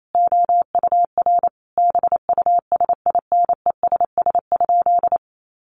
Morse code, heard as a keyed tone, about 28 wpm, 700 Hz. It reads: OUR BUSINESS?
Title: Sentences from Top 1400 Words 28wpm